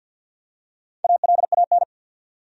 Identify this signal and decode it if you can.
Morse code, heard as a keyed tone, about 49 wpm, 700 Hz. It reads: ALAN
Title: USA Common Names - Max 5 Letters - 50wpm